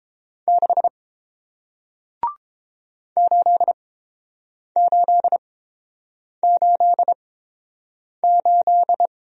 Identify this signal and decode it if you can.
Morse code, heard as a keyed tone, about 30 wpm, spaced as for 7 wpm, 700 Hz. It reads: 6 8888